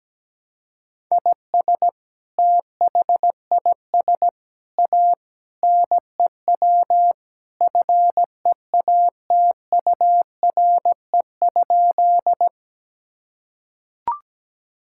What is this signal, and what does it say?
Morse code, heard as a keyed tone, about 17 wpm, 700 Hz. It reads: IS THIS A NEW FEATURE?
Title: Sentences from Top 1000 Words - Review Part 5 - 17wpm